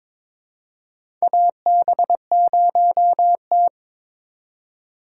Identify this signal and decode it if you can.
Morse code, heard as a keyed tone, about 22 wpm, 700 Hz. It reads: AB0T
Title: USA Call Signs - 4 Characters - 3x Character Spacing - 22wpm